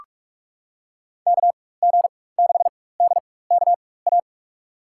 Morse code, heard as a keyed tone, about 46 wpm, spaced as for 21 wpm, 700 Hz. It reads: KG6DKA
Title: USA Call Signs - 6 Characters - 4x Character Spacing - 45wpm